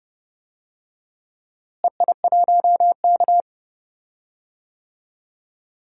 Morse code, heard as a keyed tone, about 32 wpm, 700 Hz. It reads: EI1K
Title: DXCC-100 Call Signs - 4 Characters - Repeated 2x - 30wpm